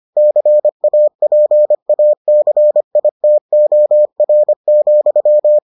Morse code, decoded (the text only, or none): CAPACITOR,